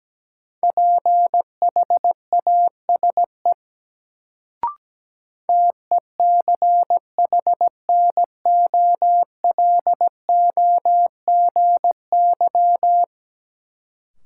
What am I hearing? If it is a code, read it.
Morse code, heard as a keyed tone, about 17 wpm, 700 Hz. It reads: PHASE TECHNOLOGY
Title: Sentences from Top 1200 Words - New Word Drill Alt - 17wpm